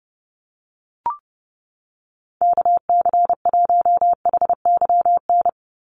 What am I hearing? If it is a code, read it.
Morse code, heard as a keyed tone, about 30 wpm, 700 Hz. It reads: KC1HYN
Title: USA Call Signs 30wpm